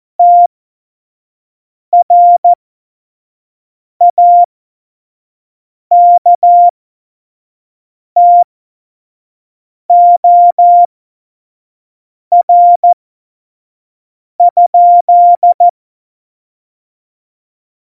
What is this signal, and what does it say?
Morse code, heard as a keyed tone, about 14 wpm, spaced as for 5 wpm, 700 Hz. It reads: TRAKTOR?
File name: Morsekurs / Witze / Witz 27. Oktober 2020, 25 F70